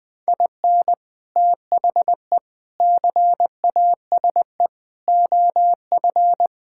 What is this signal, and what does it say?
Morse code, heard as a keyed tone, about 21 wpm, 700 Hz. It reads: IN THE CASE OF
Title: Sentences with 3-Word Phrases - 20wpm